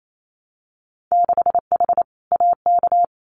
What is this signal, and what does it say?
Morse code, heard as a keyed tone, about 28 wpm, 700 Hz. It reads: 6H AK
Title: Winter Field Day Exchanges Encore 28wpm